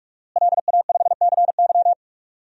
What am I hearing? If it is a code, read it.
Morse code, heard as a keyed tone, about 46 wpm, 700 Hz. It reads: LA5CY
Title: DXCC-100 Call Signs - 5 Characters - Repeated 3x - 45wpm